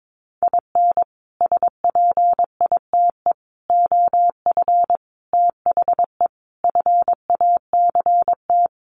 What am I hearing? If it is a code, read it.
Morse code, heard as a keyed tone, about 22 wpm, 700 Hz. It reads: IN SPITE OF THE FACT
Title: Sets of 5 Words Encore2 22wpm